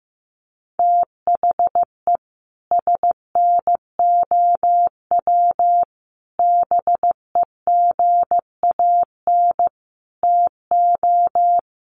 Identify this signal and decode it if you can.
Morse code, heard as a keyed tone, about 15 wpm, 700 Hz. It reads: THE SNOW BEGAN TO
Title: Sentences from Top 1500 Words 15wpm